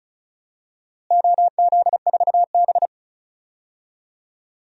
Morse code, heard as a keyed tone, about 36 wpm, 700 Hz. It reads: OZ4B
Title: DXCC-100 Call Signs - 4 Characters - Repeated 2x - 35wpm